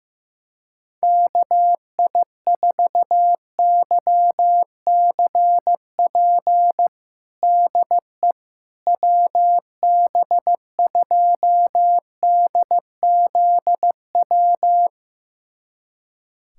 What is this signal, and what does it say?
Morse code, heard as a keyed tone, about 15 wpm, 700 Hz. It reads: KI4YCP DE WB2DZW
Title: USA Callsign DE Callsign - 6 Characters - 15wpm